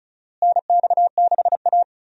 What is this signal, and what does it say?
Morse code, heard as a keyed tone, about 34 wpm, 700 Hz. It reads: NX6A
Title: USA Call Signs - 4 Characters - Repeated 3x - 35wpm